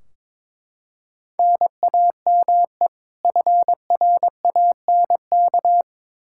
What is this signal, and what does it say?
Morse code, heard as a keyed tone, about 22 wpm, 700 Hz. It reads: NAME FRANK
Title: K1USN SST QSO Practice 22wpm